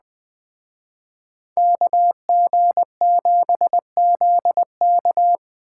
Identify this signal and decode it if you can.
Morse code, heard as a keyed tone, about 20 wpm, 700 Hz. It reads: KG7ZK